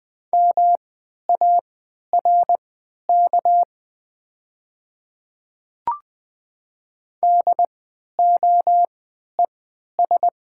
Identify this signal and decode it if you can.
Morse code, heard as a keyed tone, about 20 wpm, spaced as for 11 wpm, 690 Hz. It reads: MARK DOES
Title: 3x Character Spacing - Top 300 Words - 20wpm